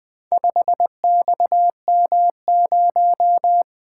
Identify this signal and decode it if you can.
Morse code, heard as a keyed tone, about 20 wpm, 700 Hz. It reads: 5XM0
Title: Speed-Racing - Four Letter-Number Combinations - 20wpm